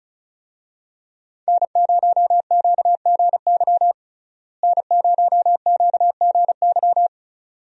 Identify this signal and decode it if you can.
Morse code, heard as a keyed tone, about 34 wpm, 700 Hz. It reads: N0QGY N0QGY
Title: USA Call Signs - 5 Characters - Repeated 2x - 35wpm